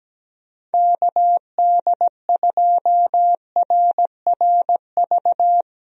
Morse code, heard as a keyed tone, about 17 wpm, 700 Hz. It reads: KD2RRV
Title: USA Call Signs - 6 Characters - 17wpm